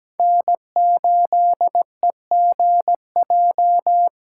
Morse code, heard as a keyed tone, about 17 wpm, 700 Hz. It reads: N8EGJ
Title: USA Callsign DE Callsign 17wpm